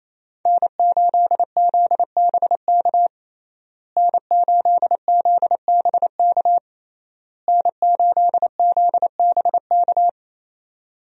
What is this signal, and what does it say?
Morse code, heard as a keyed tone, about 28 wpm, 700 Hz. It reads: N8ZBK N8ZBK N8ZBK
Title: USA Call Signs - 5 Characters - Repeated 3x - 28wpm